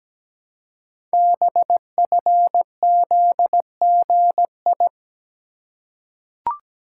Morse code, heard as a keyed tone, about 17 wpm, 700 Hz. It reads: BFZGI
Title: Code Groups - 5 Letters - 17wpm